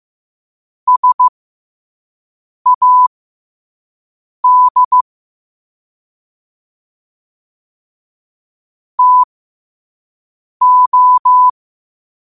Morse code, heard as a keyed tone, about 15 wpm, spaced as for 5 wpm, 1000 Hz. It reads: SAD TO
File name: qotdfiles / 5wpm